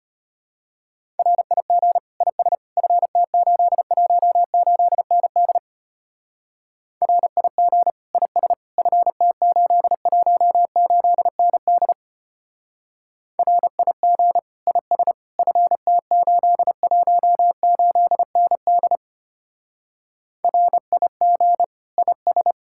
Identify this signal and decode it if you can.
Morse code, heard as a keyed tone, about 31 wpm, 700 Hz. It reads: RIG IS FT818ND RIG IS FT818ND RIG IS FT818ND RIG IS
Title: Speed-Racing Common QSO Phrases - 25wpm